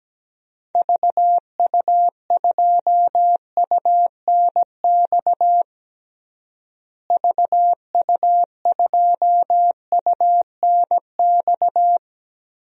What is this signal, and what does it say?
Morse code, heard as a keyed tone, about 17 wpm, 700 Hz. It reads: VU2UNX VU2UNX
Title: DXCC-100 Call Signs - 6 Characters - Repeated 3x - 17wpm